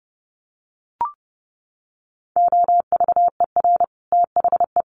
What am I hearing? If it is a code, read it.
Morse code, heard as a keyed tone, about 30 wpm, 700 Hz. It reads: OVER THE
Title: Sets of 5 Words Encore 30wpm